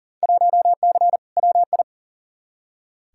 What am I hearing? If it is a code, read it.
Morse code, heard as a keyed tone, about 40 wpm, 700 Hz. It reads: 1C WI
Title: Field Day Exchanges - 40wpm